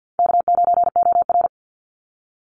Morse code, heard as a keyed tone, about 50 wpm, 700 Hz. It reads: K9OR
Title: Easy Call Signs - 50wpm